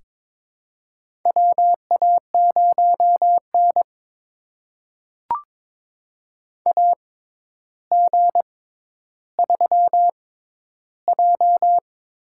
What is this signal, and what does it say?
Morse code, heard as a keyed tone, about 21 wpm, spaced as for 11 wpm, 700 Hz. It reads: WA0N AG3J
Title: USA Call Signs - 4 Characters - 6x Character Spacing - 22wpm